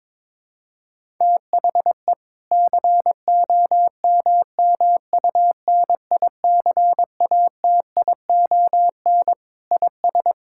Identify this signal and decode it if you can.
Morse code, heard as a keyed tone, about 22 wpm, 700 Hz. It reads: THE COMMUNICATION IS
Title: Sentences from Top 1000 Words - Review Part 6 - 22wpm